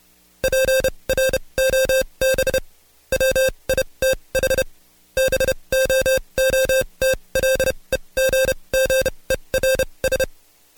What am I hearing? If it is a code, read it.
Morse code, heard as a keyed tone, about 29 wpm, spaced as for 23 wpm, 530 Hz. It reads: PROB WITH BOOTLEGGERS
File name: Code Quick / AudioDownld2 / Seg138-Stepper11 23WPM